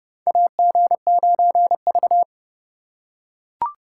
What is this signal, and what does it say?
Morse code, heard as a keyed tone, about 30 wpm, 700 Hz. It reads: AG9V
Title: USA Call Signs - 4x Character Spacing - 30wpm